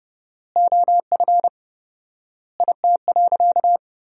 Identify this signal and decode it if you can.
Morse code, heard as a keyed tone, about 30 wpm, 700 Hz. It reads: OF IT.
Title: 4x Word-Spacing - Sentences from Top 200 Words - 30wpm